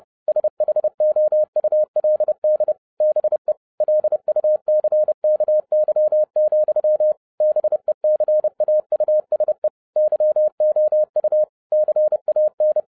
Morse code, heard as a keyed tone, about 30 wpm, 610 Hz. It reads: SHOULD BE LUCKY, BECAUSE YOU CAN
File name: 2011-07-mp3 / n60 BACK FROM A BIG ACCIDENT 30wpm